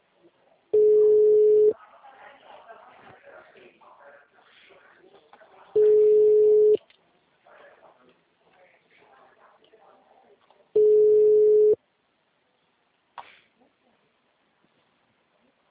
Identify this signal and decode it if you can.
Ringback tone (Europe, 425 Hz), with speech.